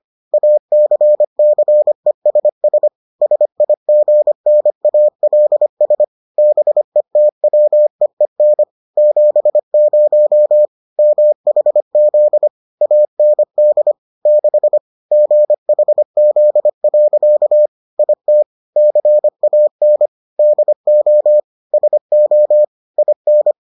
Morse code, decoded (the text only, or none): ACCESS SIGNALS BETWEEN 70 MHZ AND 6 GHZ. IT CAN DO SO IN